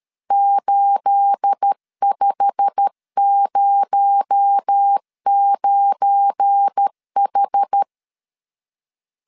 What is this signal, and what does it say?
Morse code, heard as a keyed tone, about 13 wpm, 800 Hz. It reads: 8509H
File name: CW / DET / 1 9 1500C S C 12wpm 3811